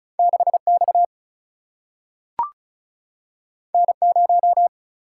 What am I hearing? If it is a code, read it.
Morse code, heard as a keyed tone, about 36 wpm, 700 Hz. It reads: 6 X N 0
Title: Single-Letter Single-Number 35wpm